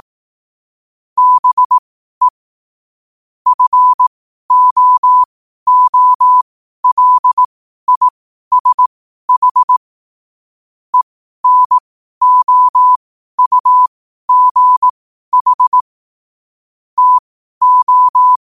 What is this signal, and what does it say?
Morse code, heard as a keyed tone, about 18 wpm, spaced as for 12 wpm, 1000 Hz. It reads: BE FOOLISH ENOUGH TO
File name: qotdfiles / 12wpm